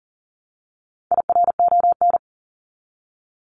Morse code, heard as a keyed tone, about 40 wpm, 700 Hz. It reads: IRON
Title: Top 1751-2000 Words 40wpm